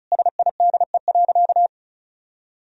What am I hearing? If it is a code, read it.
Morse code, heard as a keyed tone, about 36 wpm, 700 Hz. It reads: SIDE.
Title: Sentences from Top 1000 Words - Review Part 1 - 35wpm